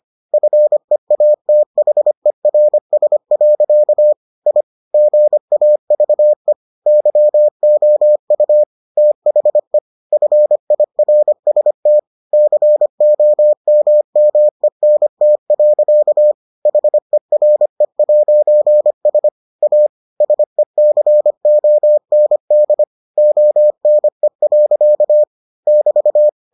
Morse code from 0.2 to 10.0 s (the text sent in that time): FEATHERS. I GAVE YOU THE